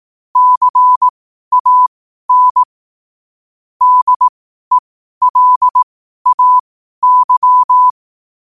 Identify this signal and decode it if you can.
Morse code, heard as a keyed tone, about 18 wpm, spaced as for 12 wpm, 1000 Hz. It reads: CAN DELAY
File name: qotdfiles / 12wpm